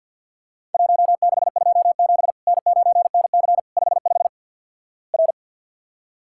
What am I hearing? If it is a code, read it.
Morse code, heard as a keyed tone, about 48 wpm, 700 Hz. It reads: 1627 N9NC 55 R
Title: Worked All Europe DX Contest - QTC Exchange - 50wpm